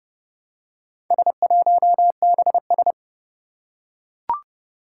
Morse code, heard as a keyed tone, about 30 wpm, 700 Hz. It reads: S1BS
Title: Four Challenging Letter-Number Combinations 30wpm